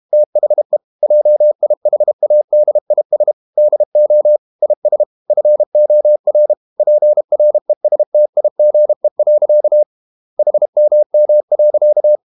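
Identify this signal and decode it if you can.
Morse code, heard as a keyed tone, about 32 wpm, 600 Hz. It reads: THE JIHADIS DO IS FOR PRESTIGE. HMM.
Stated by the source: Morse code practice at 32 wpm from CW med Gnister, reading Ken Follett's novel Never in English.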